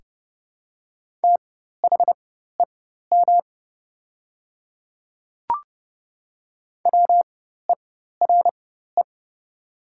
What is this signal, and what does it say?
Morse code, heard as a keyed tone, about 30 wpm, spaced as for 14 wpm, 700 Hz. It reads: THEM WERE
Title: 4x Character Spacing - Top 100 Words - 30wpm